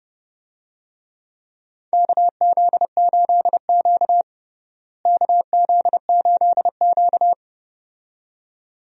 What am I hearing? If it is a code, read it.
Morse code, heard as a keyed tone, about 32 wpm, 700 Hz. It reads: KZ8Q KZ8Q
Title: USA Call Signs - 4 Characters - Repeated 2x - 30wpm